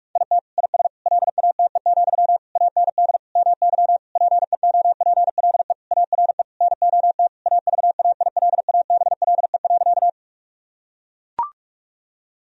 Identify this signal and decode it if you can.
Morse code, heard as a keyed tone, about 41 wpm, 700 Hz. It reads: IT IS LATE, AND MY PEOPLE ARE NOT AVAILABLE.
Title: Sentences from Top 1000 Words - Review Part 3 - 45wpm